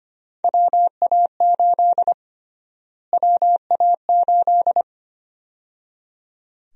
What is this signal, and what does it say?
Morse code, heard as a keyed tone, about 25 wpm, 700 Hz. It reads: WA8 WA8
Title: USA Call Sign Prefixes - Repeated 2x - 25wpm